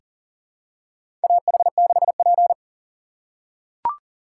Morse code, heard as a keyed tone, about 40 wpm, 700 Hz. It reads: AH6P
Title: USA Call Signs - 2x Character Spacing - 40wpm